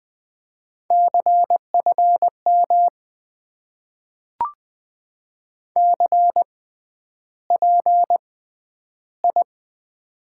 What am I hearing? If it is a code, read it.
Morse code, heard as a keyed tone, about 20 wpm, spaced as for 10 wpm, 700 Hz. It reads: CFM CPI
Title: Top 90 QSO Elements - 6x Character Spacing - 20wpm